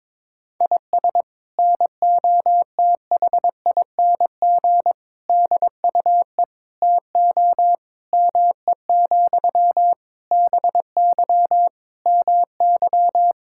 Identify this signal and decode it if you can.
Morse code, heard as a keyed tone, about 22 wpm, 700 Hz. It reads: IS NOTHING DUE TO ME, BY MY
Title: Sentences from Top 1000 Words - Review Part 6 - 22wpm